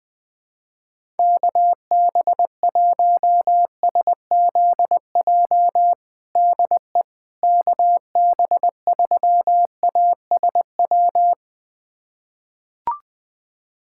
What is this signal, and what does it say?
Morse code, heard as a keyed tone, about 21 wpm, 700 Hz. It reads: KB1SZJ DE KB3ASW
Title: USA Callsign DE Callsign - 6 Characters - 20wpm